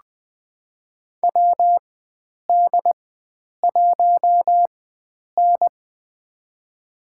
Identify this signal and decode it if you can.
Morse code, heard as a keyed tone, about 20 wpm, spaced as for 9 wpm, 700 Hz. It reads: WD1N
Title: USA Call Signs - 4 Characters - 4x Character Spacing - 20wpm